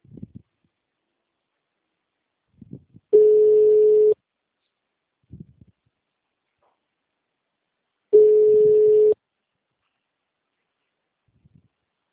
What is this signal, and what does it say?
Ringback tone (Europe, 425 Hz).